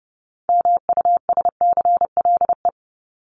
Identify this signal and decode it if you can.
Morse code, heard as a keyed tone, about 30 wpm, 700 Hz. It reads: MUSCLE
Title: Top 1251-1500 Words 30wpm